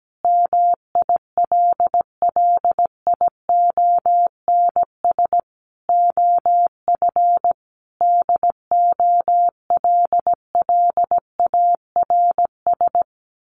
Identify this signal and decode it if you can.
Morse code, heard as a keyed tone, about 17 wpm, 700 Hz. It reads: MILLIONS OF DOLLARS
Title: Sets of 5 Words Encore 17wpm